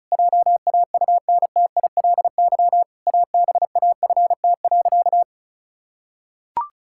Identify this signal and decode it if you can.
Morse code, heard as a keyed tone, about 34 wpm, 700 Hz. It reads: JAUNTILY ABAFT.
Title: Pangram Challenge v2 - 35wpm